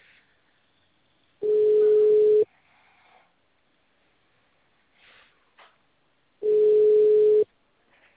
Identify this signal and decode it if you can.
Ringback tone (Europe, 425 Hz).